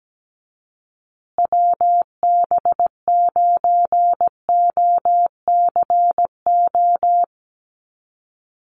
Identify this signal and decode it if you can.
Morse code, heard as a keyed tone, about 17 wpm, 700 Hz. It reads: WB9OCO
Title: USA Call Signs Repeated 3x 17wpm